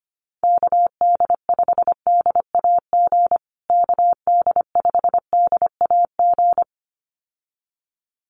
Morse code, heard as a keyed tone, about 25 wpm, 700 Hz. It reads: KD5DAG KD5DAG
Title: USA Call Signs Repeated 2x 25wpm